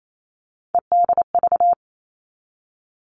Morse code, heard as a keyed tone, about 28 wpm, 700 Hz. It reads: EDV
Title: All Cut Numbers - 3-Digit 28wpm